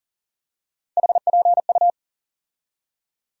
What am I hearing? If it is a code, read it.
Morse code, heard as a keyed tone, about 40 wpm, 700 Hz. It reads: HPU
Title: USA Call Sign Postfixes - 3 Letters 40wpm